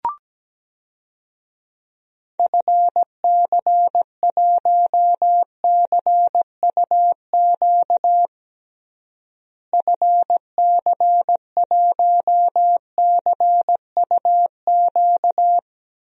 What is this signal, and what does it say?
Morse code, heard as a keyed tone, about 17 wpm, 700 Hz. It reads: FC1CUQ FC1CUQ
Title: DXCC-100 Call Signs - 6 Characters - Repeated 3x - 17wpm